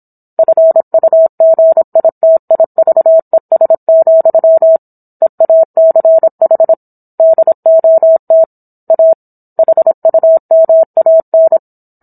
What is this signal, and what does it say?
Morse code, heard as a keyed tone, about 27 wpm, 650 Hz. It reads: FUGITIVES, EACH DOT A HUMAN
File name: Book One / 17-Chapter 17 - 26 WPM